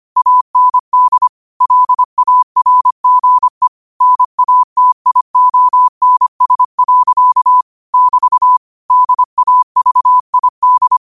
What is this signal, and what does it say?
Morse code, heard as a keyed tone, about 25 wpm, 1000 Hz. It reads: AND LARGE NATIONS. <BT> DAVID
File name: qotdfiles / 25wpm